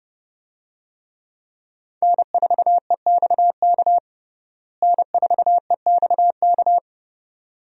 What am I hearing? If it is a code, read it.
Morse code, heard as a keyed tone, about 30 wpm, 700 Hz. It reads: N4EXK N4EXK
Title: USA Call Signs - 5 Characters - Repeated 2x - 30wpm